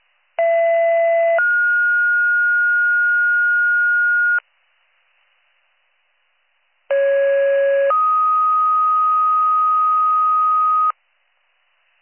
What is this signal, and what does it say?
Two-tone page, heard as a one-second tone, then a three-second tone, about 670 then 1430 Hz, about 550 then 1250 Hz. It is heard from 0.4 to 10.9 s.